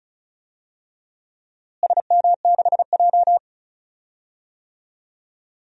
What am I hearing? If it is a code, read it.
Morse code, heard as a keyed tone, about 36 wpm, 700 Hz. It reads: SM6J
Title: DXCC-100 Call Signs - 4 Characters - Repeated 3x - 35wpm